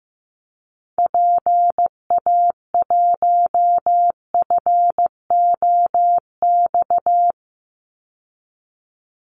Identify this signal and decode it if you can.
Morse code, heard as a keyed tone, about 15 wpm, 700 Hz. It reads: PA1FOX
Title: Call Signs Repeated 2x - 15wpm